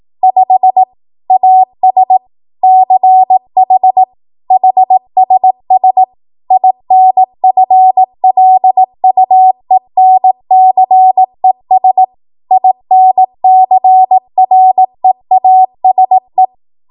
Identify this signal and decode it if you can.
Morse code, heard as a keyed tone, about 18 wpm, 750 Hz. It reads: 5 AS CH HSS INFLUENCES INCREASE
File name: Morse / 18WPM